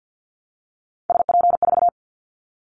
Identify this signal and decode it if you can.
Morse code, heard as a keyed tone, about 50 wpm, 700 Hz. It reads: SP4
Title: Callsign Prefixes for Top 25 DXCC - 50wpm